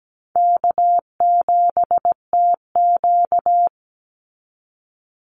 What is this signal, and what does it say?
Morse code, heard as a keyed tone, about 17 wpm, 700 Hz. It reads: K7TQ
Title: NA SOTA Chaser Callsigns 17wpm